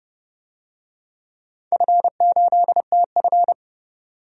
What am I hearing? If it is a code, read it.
Morse code, heard as a keyed tone, about 30 wpm, 700 Hz. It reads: F8TF
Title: DXCC-100 Call Signs - 4 Characters - 30wpm